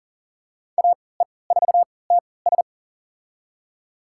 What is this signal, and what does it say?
Morse code, heard as a keyed tone, about 40 wpm, spaced as for 23 wpm, 700 Hz. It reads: AE4TS
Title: USA Call Signs - 5 Characters - 3x Character Spacing - 40wpm